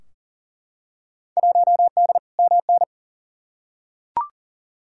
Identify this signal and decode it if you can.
Morse code, heard as a keyed tone, about 40 wpm, 700 Hz. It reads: 1D MN
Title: Field Day Exchanges - 40wpm